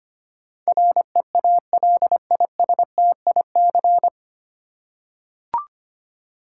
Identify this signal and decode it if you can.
Morse code, heard as a keyed tone, about 25 wpm, 700 Hz. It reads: REALISTIC
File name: practice / 25wpm / 9 Letter Words - First 3 Letters Given - 25wpm